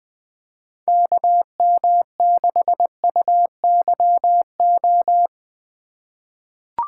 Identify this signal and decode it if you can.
Morse code, heard as a keyed tone, about 20 wpm, 700 Hz. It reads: KM6UYO
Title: USA Call Signs - 2x Character Spacing - 20wpm